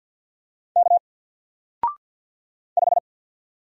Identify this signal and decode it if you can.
Morse code, heard as a keyed tone, about 49 wpm, 700 Hz. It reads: K 5
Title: Single Letter-Number Alt 50wpm